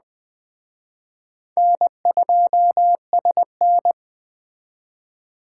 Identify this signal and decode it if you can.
Morse code, heard as a keyed tone, about 20 wpm, 700 Hz. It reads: N2SN
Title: Speed Racing - USA 1x2 Call Signs 17wpm